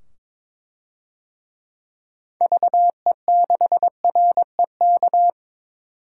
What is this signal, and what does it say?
Morse code, heard as a keyed tone, about 21 wpm, 700 Hz. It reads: VE6REK
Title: DXCC-100 Call Signs - 6 Characters - 22wpm